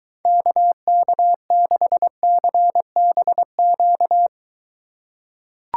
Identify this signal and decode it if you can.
Morse code, heard as a keyed tone, about 23 wpm, 700 Hz. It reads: KK6CBQ